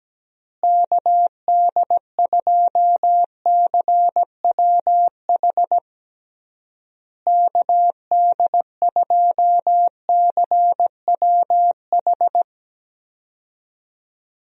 Morse code, heard as a keyed tone, about 17 wpm, 700 Hz. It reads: KD2CWH KD2CWH
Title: USA Call Signs - 6 Characters - Repeated 3x - 17wpm